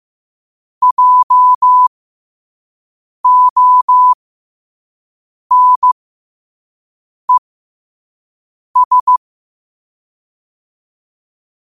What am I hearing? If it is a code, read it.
Morse code, heard as a keyed tone, about 15 wpm, spaced as for 5 wpm, 1000 Hz. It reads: JONES
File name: 5wpm